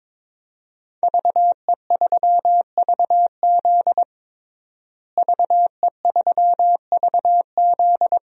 Morse code, heard as a keyed tone, about 22 wpm, 700 Hz. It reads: VE3VZ VE3VZ
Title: DXCC-100 Call Signs - 5 Characters - Repeated 2x - 22wpm